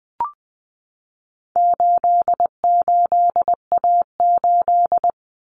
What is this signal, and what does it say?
Morse code, heard as a keyed tone, about 20 wpm, 700 Hz. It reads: 88A8
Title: Common Cut Numbers - 4-Digit 20wpm